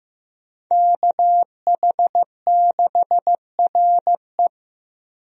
Morse code, heard as a keyed tone, about 15 wpm, 700 Hz. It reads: KH6RE
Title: USA Call Signs - 5 Characters - Repeated 2x - 15wpm